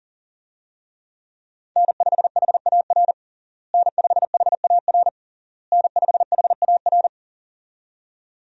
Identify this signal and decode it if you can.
Morse code, heard as a keyed tone, about 40 wpm, 700 Hz. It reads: N5HAR N5HAR N5HAR
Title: USA Call Signs - 5 Characters - Repeated 3x - 40wpm